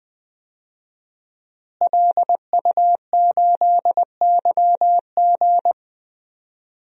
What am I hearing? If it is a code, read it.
Morse code, heard as a keyed tone, about 20 wpm, 700 Hz. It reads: LU8YG
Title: DXCC-100 Call Signs - 5 Characters - Repeated 3x - 20wpm